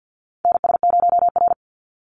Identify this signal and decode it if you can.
Morse code, heard as a keyed tone, about 50 wpm, 700 Hz. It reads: NS0R
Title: Call Signs - 50wpm